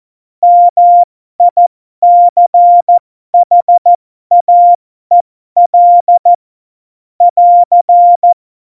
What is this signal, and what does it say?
Morse code, heard as a keyed tone, about 14 wpm, spaced as for 12 wpm, 700 Hz. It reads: MICHAEL <AR>
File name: Morsekurs / Witze